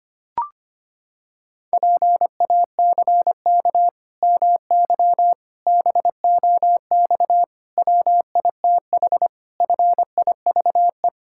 Morse code, heard as a keyed tone, about 25 wpm, 700 Hz. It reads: PACK MY BOX WITH FIVE
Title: Pangram Challenge v2 - 25wpm